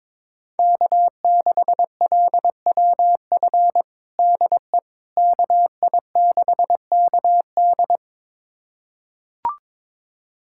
Morse code, heard as a keyed tone, about 22 wpm, 700 Hz. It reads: K6LWF DE KI6KD
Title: USA Callsign DE Callsign - 5 Characters - 22wpm